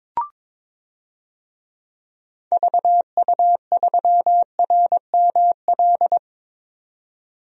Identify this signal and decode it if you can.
Morse code, heard as a keyed tone, about 21 wpm, 700 Hz. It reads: VU3RML
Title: DXCC-100 Call Signs - 6 Characters - 22wpm